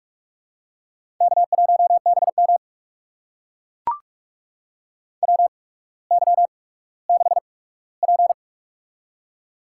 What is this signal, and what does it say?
Morse code, heard as a keyed tone, about 44 wpm, spaced as for 19 wpm, 700 Hz. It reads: K1BM WY6P
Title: USA Call Signs - 8x Character Spacing - 45wpm